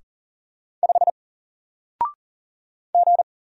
Morse code, heard as a keyed tone, about 40 wpm, 700 Hz. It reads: 5 G
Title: Single Letter-Number Alt 40wpm